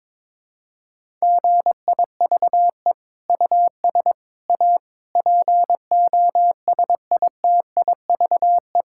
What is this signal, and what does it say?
Morse code, heard as a keyed tone, about 21 wpm, 700 Hz. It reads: GIVE US A POSITIVE
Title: Sentences from Top 1600 Words 22wpm